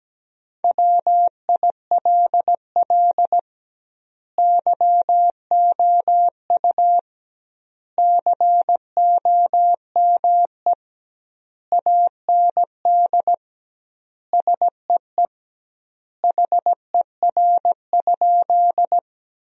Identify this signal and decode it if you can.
Morse code, heard as a keyed tone, about 17 wpm, 700 Hz. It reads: WILL YOU COME AND SEE HER?
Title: 2x Word-Spacing - Sentences from Top 100 Words - 17wpm